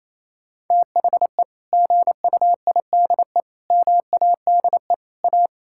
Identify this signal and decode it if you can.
Morse code, heard as a keyed tone, about 27 wpm, 700 Hz. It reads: THE GUIDE MADE A
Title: Sentences from Top 1000 Words - Review Part 6 - 28wpm